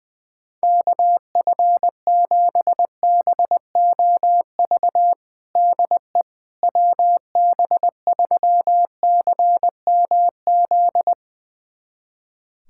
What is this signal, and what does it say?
Morse code, heard as a keyed tone, about 20 wpm, 700 Hz. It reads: KF7BOV DE WB3CMZ
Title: USA Callsign DE Callsign 20wpm